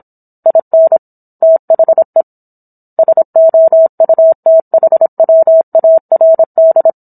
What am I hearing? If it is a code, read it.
Morse code, heard as a keyed tone, about 27 wpm, 650 Hz. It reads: IN THE SOUTHWARD